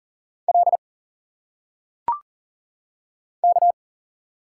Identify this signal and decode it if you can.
Morse code, heard as a keyed tone, about 40 wpm, 700 Hz. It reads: L K
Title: Single Letters - Phonetic Pronunciation 40wpm